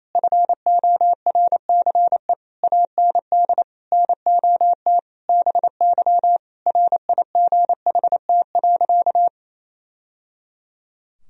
Morse code, heard as a keyed tone, about 28 wpm, 700 Hz. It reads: FORCE AND NOT BY RIGHT.
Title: Sentences from Top 1000 Words - Review Part 3 - 28wpm